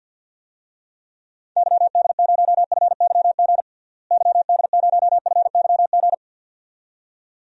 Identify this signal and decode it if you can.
Morse code, heard as a keyed tone, about 47 wpm, 700 Hz. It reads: YD0FYG YD0FYG
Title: DXCC-100 Call Signs - 6 Characters - Repeated 2x - 50wpm